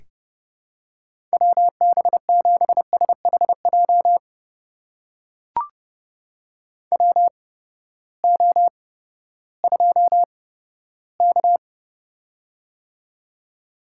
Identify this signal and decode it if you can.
Morse code, heard as a keyed tone, about 30 wpm, spaced as for 15 wpm, 700 Hz. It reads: WB7SHJ WO2K
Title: USA Call Signs - 8x Character Spacing - 30wpm